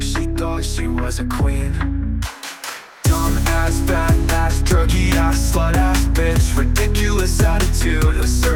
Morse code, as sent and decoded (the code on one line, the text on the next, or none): none